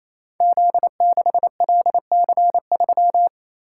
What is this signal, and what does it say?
Morse code, heard as a keyed tone, about 28 wpm, 700 Hz. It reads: Z6LC3
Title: Code Groups - 5 Letter:Numbers - 28wpm